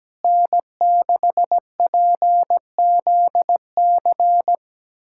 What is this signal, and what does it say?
Morse code, heard as a keyed tone, about 17 wpm, 700 Hz. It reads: N6PZC
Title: USA Callsign DE Callsign 17wpm